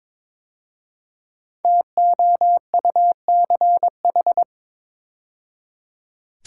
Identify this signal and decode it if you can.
Morse code, heard as a keyed tone, about 22 wpm, 700 Hz. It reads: TOUCH
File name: 5 Letter Words - First 3 Letters Given - 22wpm